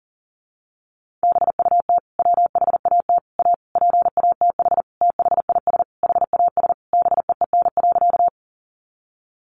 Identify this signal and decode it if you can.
Morse code, heard as a keyed tone, about 40 wpm, 700 Hz. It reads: BUT WHAT A PATH THIS HAS BEEN.
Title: Sentences from Top 1400 Words 40wpm